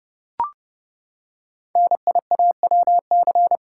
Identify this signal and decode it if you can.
Morse code, heard as a keyed tone, about 30 wpm, 690 Hz. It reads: NIAWC
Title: Code Groups - 5 Letters - 30wpm